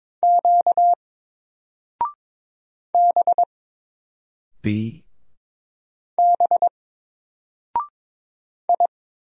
Morse code, heard as a keyed tone, about 22 wpm, 700 Hz. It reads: Q B B I
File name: Single-Letter-Number-22wpm